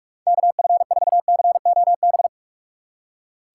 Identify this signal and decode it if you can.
Morse code, heard as a keyed tone, about 44 wpm, 700 Hz. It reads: KF4COB
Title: USA Call Signs - 6 Characters - 2x Character Spacing - 45wpm